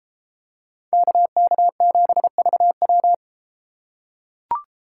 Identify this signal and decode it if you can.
Morse code, heard as a keyed tone, about 32 wpm, 700 Hz. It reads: KK7VW